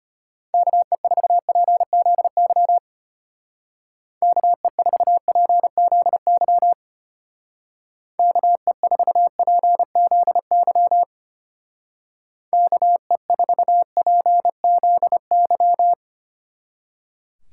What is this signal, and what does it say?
Morse code, heard as a keyed tone, about 32 wpm, 700 Hz. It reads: KE4PZY KE4PZY KE4PZY KE4PZY